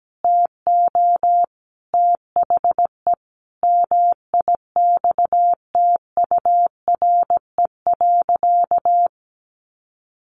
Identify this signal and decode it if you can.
Morse code, heard as a keyed tone, about 17 wpm, 700 Hz. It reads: TO THE MIXTURE.
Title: Sentences from Top 1500 Words 17wpm